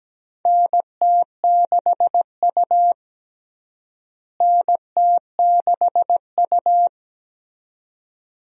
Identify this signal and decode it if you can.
Morse code, heard as a keyed tone, about 17 wpm, 700 Hz. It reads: NT6U NT6U
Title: USA Call Signs - 4 Characters - Repeated 2x - 17wpm